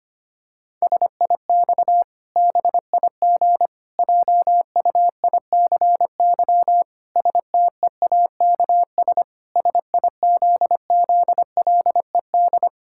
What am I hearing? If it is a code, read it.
Morse code, heard as a keyed tone, about 25 wpm, 700 Hz. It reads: SIX BIG JUICY STEAKS SIZZLED
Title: Pangram Challenge v2 - 25wpm